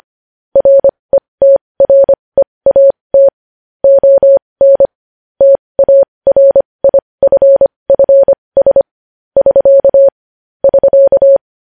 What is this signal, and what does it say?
Morse code, heard as a keyed tone, about 25 wpm, spaced as for 20 wpm, 550 Hz. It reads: RETREAT ON TARIFFS <SK> <SK>
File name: podcast / News headlines-1x-25x20